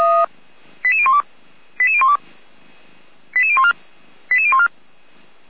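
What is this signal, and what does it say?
Five-tone selective call (ZVEI), heard as quick runs of five tones, tones of about 70 ms. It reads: 88012 88012 88014 88014
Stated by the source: Five-tone ZVEI selective call received on an SDR.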